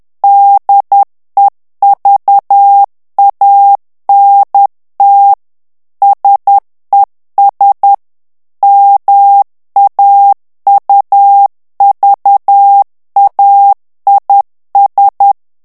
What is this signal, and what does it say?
Morse code, heard as a keyed tone, about 10 wpm, 800 Hz. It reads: DEVANT SES MAUVAIS
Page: Français - texte en clair 1 - vitesse : 10 mots minute